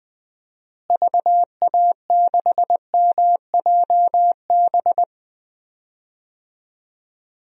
Morse code, heard as a keyed tone, about 20 wpm, 700 Hz. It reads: VA6MJB
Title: DXCC-100 Call Signs - 6 Characters - Repeated 2x - 20wpm